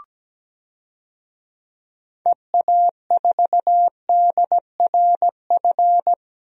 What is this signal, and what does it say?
Morse code, heard as a keyed tone, about 17 wpm, 700 Hz. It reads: EA4DRF
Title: DXCC-100 Call Signs - 6 Characters - 17wpm